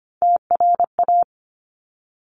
Morse code, heard as a keyed tone, about 25 wpm, 700 Hz. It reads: TRA
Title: Top Three Letter Combinations in Words 25wpm